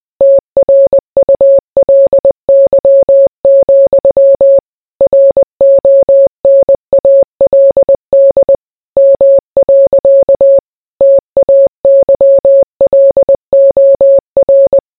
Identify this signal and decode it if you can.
Morse code, heard as a keyed tone, about 20 wpm, 550 Hz. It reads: TRULY, RONALD M. TAYLOR